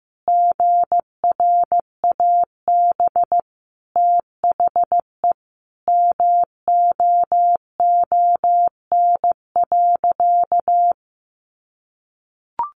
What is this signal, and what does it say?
Morse code, heard as a keyed tone, about 15 wpm, 700 Hz. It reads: GRAB THE MOON.
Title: A Sound Of Thunder 15wpm